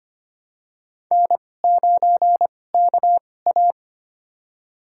Morse code, heard as a keyed tone, about 25 wpm, spaced as for 18 wpm, 700 Hz. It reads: N9KA
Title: USA Call Signs - 4 Characters - 2x Character Spacing - 25wpm